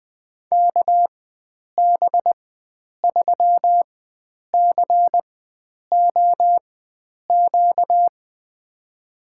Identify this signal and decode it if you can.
Morse code, heard as a keyed tone, about 20 wpm, spaced as for 9 wpm, 700 Hz. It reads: KB3COQ
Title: USA Call Signs - 6 Characters - 4x Character Spacing - 20wpm